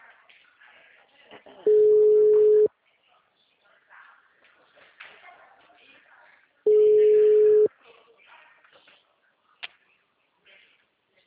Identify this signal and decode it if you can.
Ringback tone (Europe, 425 Hz), with speech.
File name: out- -2012- - - .10